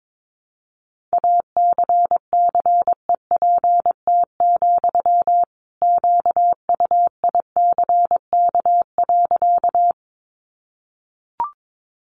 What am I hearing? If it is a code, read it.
Morse code, heard as a keyed tone, about 22 wpm, 700 Hz. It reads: ACCEPT, QUICK.
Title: Sentences from Top 1300 Words 22wpm